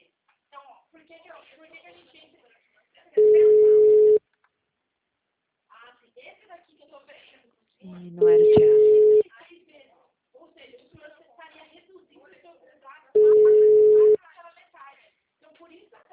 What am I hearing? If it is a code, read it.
Ringback tone (Europe, 425 Hz), with speech.